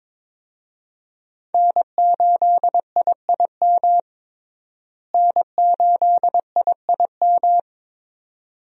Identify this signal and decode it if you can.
Morse code, heard as a keyed tone, about 22 wpm, 700 Hz. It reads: N8IIM N8IIM
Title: USA Call Signs - 5 Characters - Repeated 2x - 22wpm